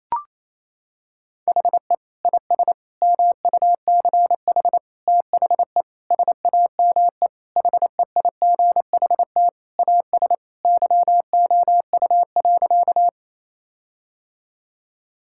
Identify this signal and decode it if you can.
Morse code, heard as a keyed tone, about 28 wpm, 700 Hz. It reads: HE IS MUCH THE SAME HEIGHT AS YOU.
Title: Sentences from Top 1600 Words 28wpm